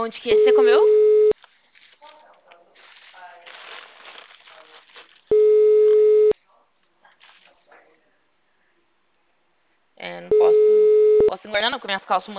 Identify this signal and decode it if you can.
Ringback tone (Europe, 425 Hz), with speech.
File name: out- -2005- - - .11